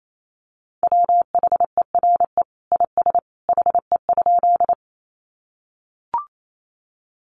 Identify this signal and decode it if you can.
Morse code, heard as a keyed tone, about 28 wpm, 700 Hz. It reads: WHERE IS HE?
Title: Sentences from Top 200 Words 28wpm